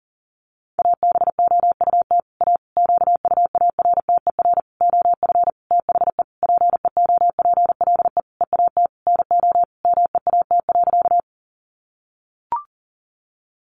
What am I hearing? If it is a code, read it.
Morse code, heard as a keyed tone, about 40 wpm, 700 Hz. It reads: ABOUT A QUARTER OF THE PEOPLE EAT NO MEAT.
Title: Sentences from Top 1300 Words 40wpm